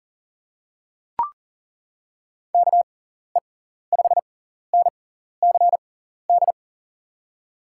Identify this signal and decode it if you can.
Morse code, heard as a keyed tone, about 40 wpm, spaced as for 14 wpm, 700 Hz. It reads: KE5NCD
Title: USA Call Signs - 6 Characters - 6x Character Spacing - 40wpm